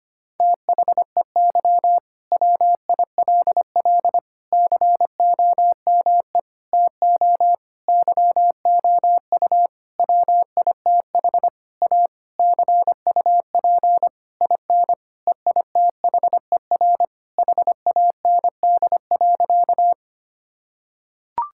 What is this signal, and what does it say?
Morse code, heard as a keyed tone, about 25 wpm, 700 Hz. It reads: THEY WILL COME TO YOU WITH A CUP IN EITHER HAND.
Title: Sentences from Top 1600 Words 25wpm